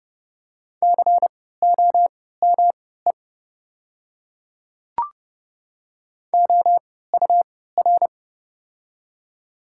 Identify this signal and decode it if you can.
Morse code, heard as a keyed tone, about 29 wpm, spaced as for 17 wpm, 700 Hz. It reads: COME OUR
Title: 3x Character Spacing - Top 500 Words - 30wpm